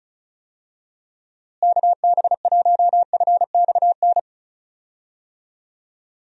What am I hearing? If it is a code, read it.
Morse code, heard as a keyed tone, about 34 wpm, 700 Hz. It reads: KB1FXN
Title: USA Call Signs - 6 Characters - Repeated 3x - 35wpm